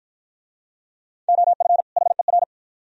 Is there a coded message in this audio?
Yes: Morse code from 1.3 to 2.4 s, heard as a keyed tone, about 52 wpm, 700 Hz. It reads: OF HER